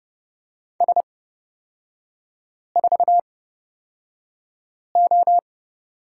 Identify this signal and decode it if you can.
Morse code, heard as a keyed tone, about 30 wpm, spaced as for 5 wpm, 700 Hz. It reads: S4O
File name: Lesson 36e - BT - Three Characters 30wpm